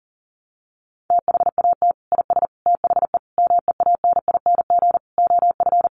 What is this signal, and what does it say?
Morse code, heard as a keyed tone, about 40 wpm, 700 Hz. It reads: THAT IS THE MEANING OF
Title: Sentences from Top 900 Words 40wpm